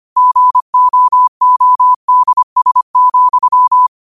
Morse code, heard as a keyed tone, about 25 wpm, 1000 Hz. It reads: GOODS,
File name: qotdfiles / 25wpm